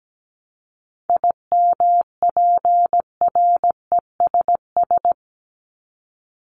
Morse code, heard as a keyed tone, about 17 wpm, 700 Hz. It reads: IMPRESS
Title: Top 1501-1750 Words 17wpm